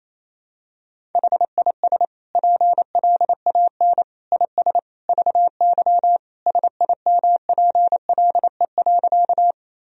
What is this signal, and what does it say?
Morse code, heard as a keyed tone, about 28 wpm, 700 Hz. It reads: HIS PLAN IS VY SIMPLE.
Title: Sentences from Top 500 Words and Abbreviations 28wpm